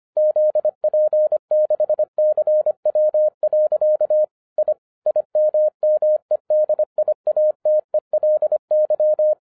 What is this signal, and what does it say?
Morse code, heard as a keyed tone, about 25 wpm, 610 Hz. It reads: ZP6CW. I IMMEDIATELY
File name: 2012-01-mp3 / n06-FIRST FISTS CONTACT-25wpm